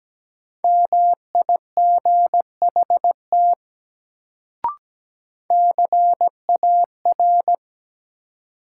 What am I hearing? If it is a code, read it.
Morse code, heard as a keyed tone, about 17 wpm, 700 Hz. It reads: MIGHT CAR
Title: Sentences from Top 300 Words - New Word Drill - 17wpm